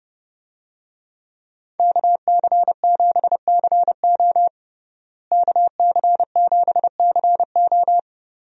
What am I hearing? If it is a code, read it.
Morse code, heard as a keyed tone, about 32 wpm, 700 Hz. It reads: KC7CO KC7CO
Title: USA Call Signs - 5 Characters - Repeated 2x - 30wpm